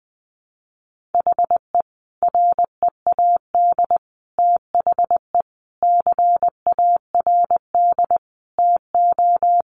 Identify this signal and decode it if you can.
Morse code, heard as a keyed tone, about 20 wpm, 700 Hz. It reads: HE READ THE CARD TO
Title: Sentences from Top 600 Words 20wpm